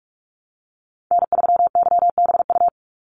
Morse code, heard as a keyed tone, about 44 wpm, 700 Hz. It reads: N3YBU
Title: USA Call Signs 45wpm